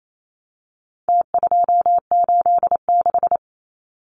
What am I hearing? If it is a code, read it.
Morse code, heard as a keyed tone, about 28 wpm, 700 Hz. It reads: T286
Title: Numbers - Contest Style - 4-Digit 28wpm